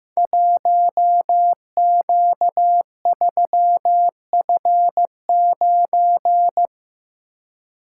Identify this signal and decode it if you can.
Morse code, heard as a keyed tone, about 15 wpm, 700 Hz. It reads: 1Q3F9
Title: Code Groups - 5 Letter:Numbers - 15wpm